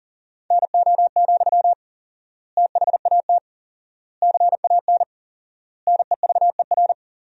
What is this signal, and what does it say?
Morse code, heard as a keyed tone, about 40 wpm, 700 Hz. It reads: NO, THAT CAN NEVER
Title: 4x Word-Spacing - Sentences from Top 300 Words - 40wpm